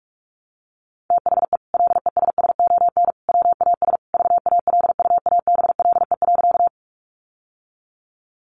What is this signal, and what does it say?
Morse code, heard as a keyed tone, about 44 wpm, 700 Hz. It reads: THE LESSON WAS VALUABLE.
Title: Sentences from Top 1200 Words 45wpm